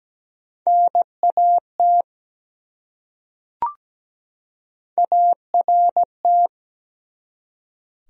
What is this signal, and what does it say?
Morse code, heard as a keyed tone, about 17 wpm, 700 Hz. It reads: NAT ART
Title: USA Common Names - Max 3 Letters - 17wpm